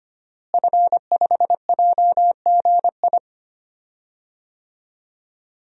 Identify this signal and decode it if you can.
Morse code, heard as a keyed tone, about 25 wpm, 700 Hz. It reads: F5JGI
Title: DXCC-100 Call Signs - 5 Characters - Repeated 2x - 25wpm